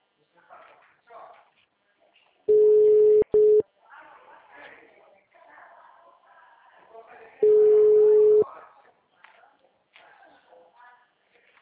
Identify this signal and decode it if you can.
Ringback tone (Europe, 425 Hz), with speech.